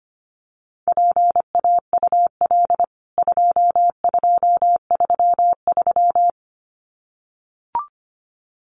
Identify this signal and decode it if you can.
Morse code, heard as a keyed tone, about 25 wpm, 700 Hz. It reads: PAUL 2233
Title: CWT Exchange 25wpm